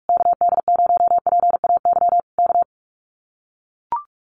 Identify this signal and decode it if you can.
Morse code, heard as a keyed tone, about 44 wpm, 700 Hz. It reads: KD0PAY K
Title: Common QSO Phrases Encore 45wpm